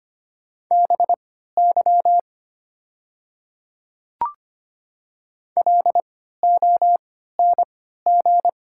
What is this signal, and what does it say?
Morse code, heard as a keyed tone, about 25 wpm, spaced as for 14 wpm, 700 Hz. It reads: BY LONG
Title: 3x Character Spacing - Top 100 Words - 25wpm